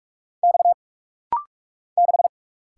Morse code, heard as a keyed tone, about 45 wpm, 700 Hz. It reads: X 6
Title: Single Letter-Number - Rapid-Fire 45wpm